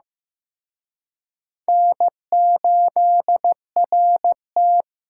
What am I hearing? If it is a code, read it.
Morse code, heard as a keyed tone, about 18 wpm, 700 Hz. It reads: N8RT
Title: Speed Racing - USA Call Signs 15wpm